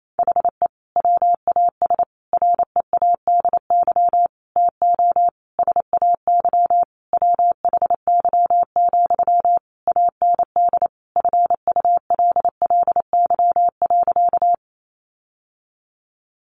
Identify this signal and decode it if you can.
Morse code, heard as a keyed tone, about 28 wpm, 700 Hz. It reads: HE WAS READY TO SAY WHY, AND FULLY.
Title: Sentences from Top 1300 Words 28wpm